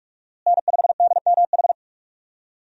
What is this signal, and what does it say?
Morse code, heard as a keyed tone, about 46 wpm, 700 Hz. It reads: N5DMH
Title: USA Call Signs - 5 Characters - Repeated 2x - 45wpm